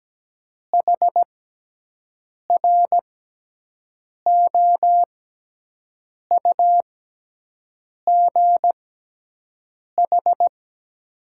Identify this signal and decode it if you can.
Morse code, heard as a keyed tone, about 17 wpm, spaced as for 6 wpm, 700 Hz. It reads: HROUGH
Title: 6x Character Spacing - Top 200 Words - 17wpm